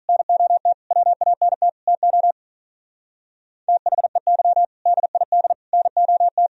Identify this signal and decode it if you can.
Morse code, heard as a keyed tone, about 47 wpm, 700 Hz. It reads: NOT WANT TO THEY DID NOT